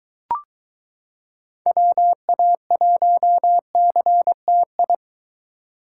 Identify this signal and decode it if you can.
Morse code, heard as a keyed tone, about 22 wpm, 700 Hz. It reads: WA1CTI